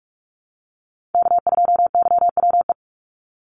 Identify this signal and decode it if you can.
Morse code, heard as a keyed tone, about 44 wpm, 700 Hz. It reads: K2YWE
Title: Call Signs - 45wpm